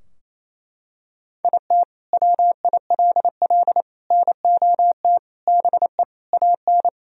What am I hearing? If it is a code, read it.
Morse code, heard as a keyed tone, about 28 wpm, 700 Hz. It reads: IT WILL NOT BE AN
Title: Sentences from Top 1000 Words - Review Part 2 - 28wpm